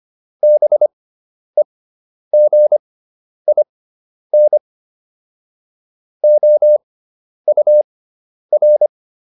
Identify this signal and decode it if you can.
Morse code, heard as a keyed tone, about 25 wpm, spaced as for 10 wpm, 600 Hz. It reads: BEGIN OUR